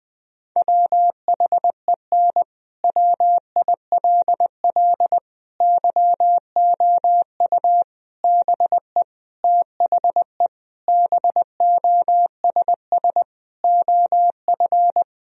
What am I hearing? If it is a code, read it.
Morse code, heard as a keyed tone, about 20 wpm, 700 Hz. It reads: WHEN WILL YOU BE THE BOSS OF
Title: Sentences from Top 1000 Words - Review Part 4 - 20wpm